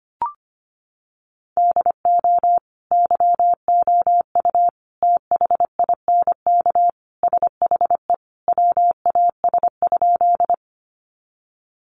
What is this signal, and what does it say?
Morse code, heard as a keyed tone, about 25 wpm, 700 Hz. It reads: DO YOU THINK SHE WAS?
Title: Sentences from Top 200 Words 25wpm